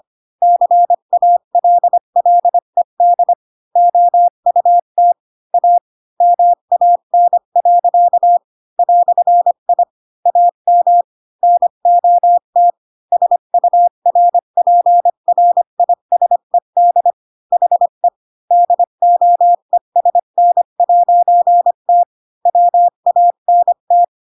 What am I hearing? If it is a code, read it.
Morse code, heard as a keyed tone, about 25 wpm, 700 Hz. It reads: CALLED OUT A MAN. "I AM NOT SURPRISED HE DOESN'T WANT
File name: Pinocchio / Pinocchio ch3 20 25 ik1hge